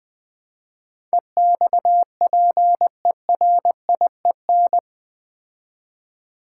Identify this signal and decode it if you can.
Morse code, heard as a keyed tone, about 20 wpm, 700 Hz. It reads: EXPERIEN
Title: Build A Word - 4 Letters At A Time - 11-Letter Words - 20wpm